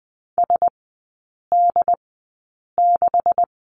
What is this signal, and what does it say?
Morse code, heard as a keyed tone, about 20 wpm, spaced as for 8 wpm, 700 Hz. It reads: SD6
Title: 152 Lesson 25e - Letter p - Three Characters 20wpm